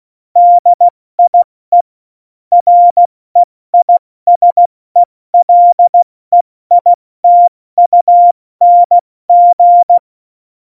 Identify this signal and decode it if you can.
Morse code, heard as a keyed tone, about 16 wpm, spaced as for 14 wpm, 700 Hz. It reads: DIE REISELEITUNG
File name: Morsekurs / Witze / Witz 11. Februar 2020, 70 F80